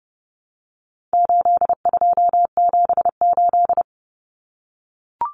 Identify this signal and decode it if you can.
Morse code, heard as a keyed tone, about 30 wpm, 700 Hz. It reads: 8278
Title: Numbers - 4-Digit 30wpm